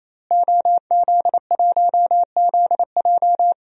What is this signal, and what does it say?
Morse code, heard as a keyed tone, about 27 wpm, 700 Hz. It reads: OZ1ZJ
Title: DXCC-100 Call Signs - 5 Characters - Repeated 3x - 28wpm